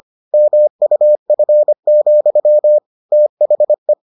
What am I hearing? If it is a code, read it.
Morse code, heard as a keyed tone, about 25 wpm, 600 Hz. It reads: MUF, THE